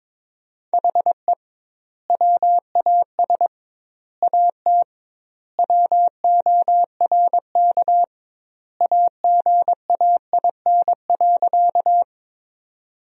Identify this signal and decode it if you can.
Morse code, heard as a keyed tone, about 21 wpm, 700 Hz. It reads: HE WAS AT WORK AGAIN.
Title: 2x Word-Spacing - Sentences from Top 200 Words - 22wpm